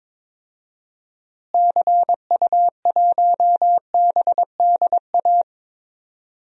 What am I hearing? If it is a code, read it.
Morse code, heard as a keyed tone, about 22 wpm, 700 Hz. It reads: CU1BDA
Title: DXCC-100 Call Signs - 6 Characters - Repeated 2x - 22wpm